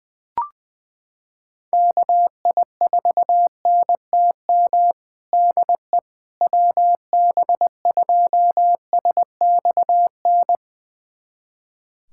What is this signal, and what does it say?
Morse code, heard as a keyed tone, about 20 wpm, 700 Hz. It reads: KI4NTM DE WB2SXN
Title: USA Callsign DE Callsign 20wpm